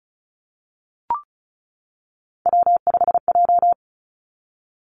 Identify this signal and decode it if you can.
Morse code, heard as a keyed tone, about 36 wpm, 700 Hz. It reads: W5J
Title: Three Challenging Letter-Number Combinations 35wpm